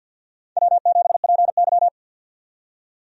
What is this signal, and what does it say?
Morse code, heard as a keyed tone, about 48 wpm, 700 Hz. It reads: W7PY
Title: USA Call Signs - 4 Characters - 50wpm